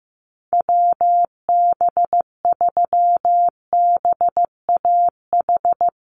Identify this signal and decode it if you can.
Morse code, heard as a keyed tone, about 15 wpm, 700 Hz. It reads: WB3BAH
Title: USA Call Signs 15wpm